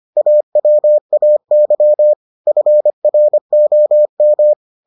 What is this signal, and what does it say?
Morse code, heard as a keyed tone, about 25 wpm, 600 Hz. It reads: AWAY FROM